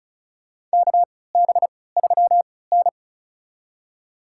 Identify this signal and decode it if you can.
Morse code, heard as a keyed tone, about 34 wpm, spaced as for 20 wpm, 700 Hz. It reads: KB3N
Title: USA Call Signs - 4 Characters - 3x Character Spacing - 35wpm